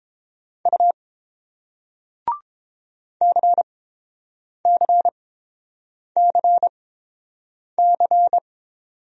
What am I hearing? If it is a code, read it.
Morse code, heard as a keyed tone, about 30 wpm, spaced as for 7 wpm, 700 Hz. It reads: U CCCC